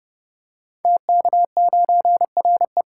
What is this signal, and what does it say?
Morse code, heard as a keyed tone, about 30 wpm, 700 Hz. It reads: TK9RE
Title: Code Groups - 5 Letter:Numbers - 30wpm